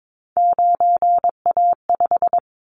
Morse code, heard as a keyed tone, about 22 wpm, 700 Hz. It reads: 9A5
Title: Callsign Prefixes for Top 25 DXCC - 22wpm